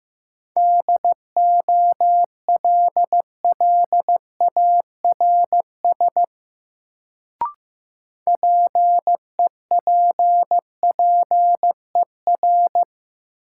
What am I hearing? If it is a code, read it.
Morse code, heard as a keyed tone, about 15 wpm, 700 Hz. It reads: DOLLARS PEPPER
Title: Sets of 5 Words - New Word Drill - 15wpm